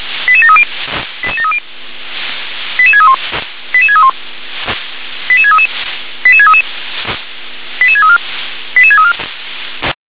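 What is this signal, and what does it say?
Five-tone selective call (ZVEI), heard as quick runs of five tones, tones of about 70 ms. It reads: 88730 88730 88631 88631 88633 88633 88634 88634
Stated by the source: ZVEI five-tone selective call received with an SDR.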